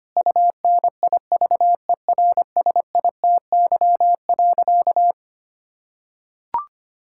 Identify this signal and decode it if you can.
Morse code, heard as a keyed tone, about 25 wpm, 700 Hz. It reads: UNIVERSITY.
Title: Sentences from Top 1000 Words - Review Part 5 - 25wpm